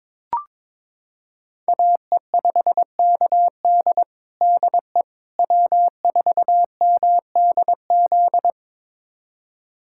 Morse code, heard as a keyed tone, about 22 wpm, 700 Hz. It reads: AE5KD DE W4MDZ
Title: USA Callsign DE Callsign - 5 Characters - 22wpm